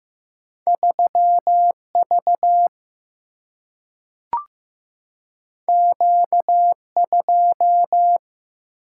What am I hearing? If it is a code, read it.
Morse code, heard as a keyed tone, about 15 wpm, 700 Hz. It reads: 3 V Q 2
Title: Single-Letter Single-Number 15wpm